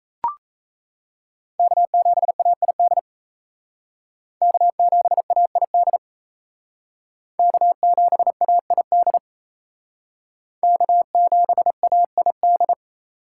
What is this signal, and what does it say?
Morse code, heard as a keyed tone, about 37 wpm, 700 Hz. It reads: K7AID K7AID K7AID K7AID